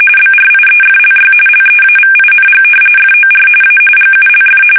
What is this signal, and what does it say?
RTTY, about 45.45 baud. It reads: BRIEFLY LEAVING FOR A GULF
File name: fsk850 / battleships